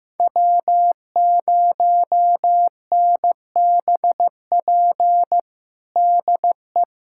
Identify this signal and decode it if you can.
Morse code, heard as a keyed tone, about 15 wpm, 700 Hz. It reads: W0NBP DE
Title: USA Callsign DE Callsign - 5 Characters - 15wpm